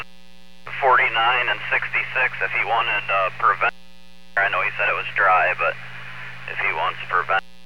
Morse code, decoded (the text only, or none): none